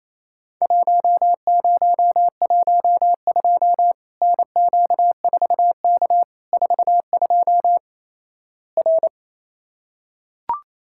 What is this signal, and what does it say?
Morse code, heard as a keyed tone, about 28 wpm, 700 Hz. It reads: 1012 NQ4K 42 R
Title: Worked All Europe DX Contest - QTC Exchange - 28wpm